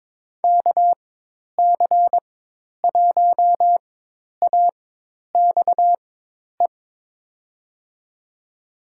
Morse code, heard as a keyed tone, about 22 wpm, spaced as for 10 wpm, 700 Hz. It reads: KC1AXE
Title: USA Call Signs - 6 Characters - 4x Character Spacing - 22wpm